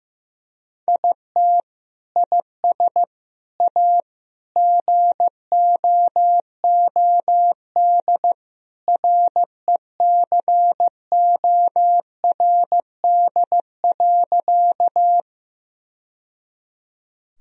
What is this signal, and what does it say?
Morse code, heard as a keyed tone, about 15 wpm, 700 Hz. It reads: IT IS A GOOD RECORD.
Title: Sentences from Top 1000 Words - Review Part 3 - 15wpm